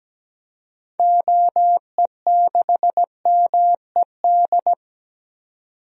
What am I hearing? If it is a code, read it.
Morse code, heard as a keyed tone, about 17 wpm, 700 Hz. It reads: OE6MED
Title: DXCC-100 Call Signs - 6 Characters - 17wpm